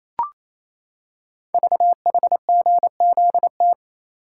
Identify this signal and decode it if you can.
Morse code, heard as a keyed tone, about 27 wpm, 700 Hz. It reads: VHGZT
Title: Code Groups - 5 Letters - 28wpm